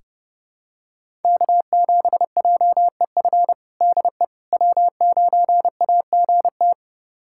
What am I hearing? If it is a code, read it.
Morse code, heard as a keyed tone, about 30 wpm, 700 Hz. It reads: K7JEF DE W9AGT
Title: USA Callsign DE Callsign - 5 Characters - 30wpm